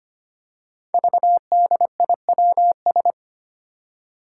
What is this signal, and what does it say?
Morse code, heard as a keyed tone, about 25 wpm, 700 Hz. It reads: VDIWS
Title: Five Challenging Letter-Number Combinations 25wpm